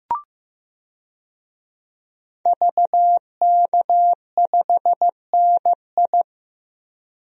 Morse code, heard as a keyed tone, about 15 wpm, 700 Hz. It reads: VK5NI
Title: DXCC-100 Call Signs - 5 Characters - Repeated 2x - 15wpm